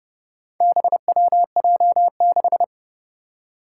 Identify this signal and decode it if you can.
Morse code, heard as a keyed tone, about 32 wpm, 700 Hz. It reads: BWJ6
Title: Four Challenging Letter-Number Combinations 30wpm